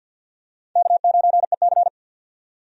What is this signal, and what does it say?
Morse code, heard as a keyed tone, about 50 wpm, 700 Hz. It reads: K9EC
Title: USA Call Signs - 4 Characters - 50wpm